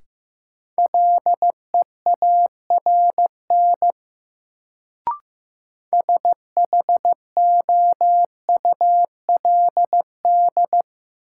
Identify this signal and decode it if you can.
Morse code, heard as a keyed tone, about 15 wpm, 700 Hz. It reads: LEARN SHOULD
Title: Sentences from Top 300 Words - New Word Drill - 15wpm